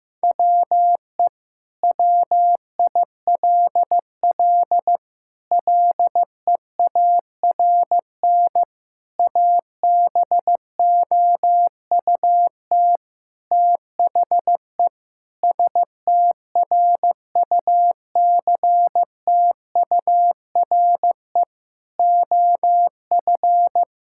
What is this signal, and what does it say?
Morse code, heard as a keyed tone, about 15 wpm, 700 Hz. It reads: WE WILL LEARN ABOUT THE STRUCTURE OF
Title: Sentences from Top 1000 Words - Review Part 6 - 15wpm